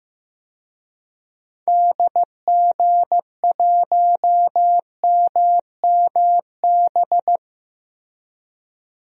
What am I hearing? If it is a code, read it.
Morse code, heard as a keyed tone, about 15 wpm, 700 Hz. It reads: DG1MMB
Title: DXCC-100 Call Signs - 6 Characters - Repeated 2x - 15wpm